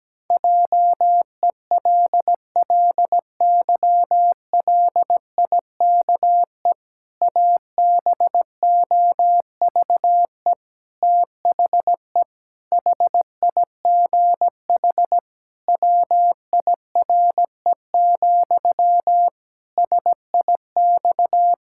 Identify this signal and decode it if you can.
Morse code, heard as a keyed tone, about 17 wpm, 700 Hz. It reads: JELLYLIKE ABOVE THE HIGH WIRE, SIX
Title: Pangram Challenge v2 - 17wpm